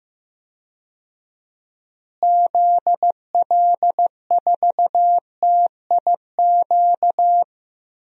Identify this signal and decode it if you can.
Morse code, heard as a keyed tone, about 15 wpm, 700 Hz. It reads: ZL4TIQ
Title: DXCC-100 Call Signs - 6 Characters - Repeated 3x - 15wpm